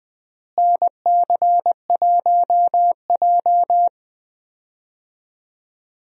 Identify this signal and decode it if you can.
Morse code, heard as a keyed tone, about 20 wpm, 700 Hz. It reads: NC1J
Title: USA Call Signs - 4 Characters - Repeated 3x - 20wpm